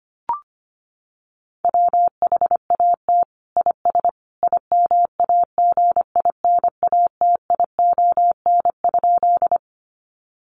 Morse code, heard as a keyed tone, about 25 wpm, 700 Hz. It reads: WHAT IS IMAGINATION?
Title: Sentences from Top 1300 Words 25wpm